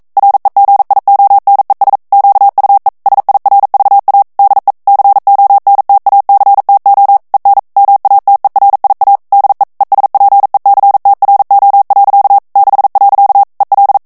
Morse code, heard as a keyed tone, about 42 wpm, 800 Hz. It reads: REGIONES QUE SIRVA DE CONTACTO EN MATERIA DE ESPECTRO. 6. EL
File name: morse mp3 / ESPAGNOL TEXTE EN CLAIR 1 VITESSE 40 MOTS MINUTES